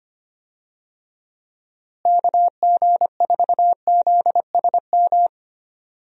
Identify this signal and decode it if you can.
Morse code, heard as a keyed tone, about 25 wpm, 700 Hz. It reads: KG4ZSM
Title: USA Call Signs - 6 Characters - Repeated 2x - 25wpm